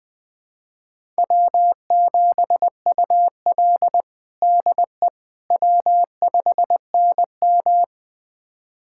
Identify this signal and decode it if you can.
Morse code, heard as a keyed tone, about 20 wpm, 700 Hz. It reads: W7UL DE W5NM
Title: USA Callsign DE Callsign - 4 Characters - 20wpm